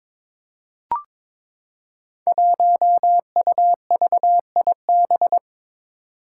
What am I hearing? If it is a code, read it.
Morse code, heard as a keyed tone, about 22 wpm, 700 Hz. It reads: 1UVIB
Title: Five Challenging Letter-Number Combinations 22wpm